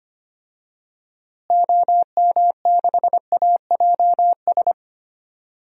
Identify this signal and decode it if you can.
Morse code, heard as a keyed tone, about 25 wpm, 700 Hz. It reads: OM6AJS
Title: DXCC-100 Call Signs - 6 Characters - Repeated 2x - 25wpm